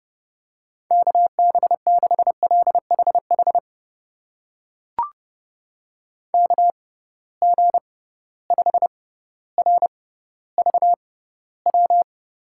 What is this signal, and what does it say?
Morse code, heard as a keyed tone, about 30 wpm, spaced as for 15 wpm, 700 Hz. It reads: KB6LHH KG5RVW
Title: USA Call Signs - 6 Characters - 6x Character Spacing - 30wpm